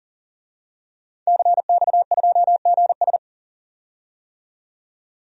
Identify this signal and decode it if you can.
Morse code, heard as a keyed tone, about 40 wpm, 700 Hz. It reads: CX2GS
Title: DXCC-100 Call Signs - 5 Characters - 40wpm